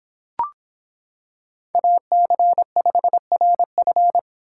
Morse code, heard as a keyed tone, about 26 wpm, 700 Hz. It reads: AC5RF